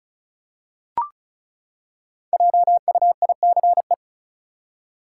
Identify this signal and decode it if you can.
Morse code, heard as a keyed tone, about 36 wpm, 700 Hz. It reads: JUICE
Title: Five Letter Words - 35wpm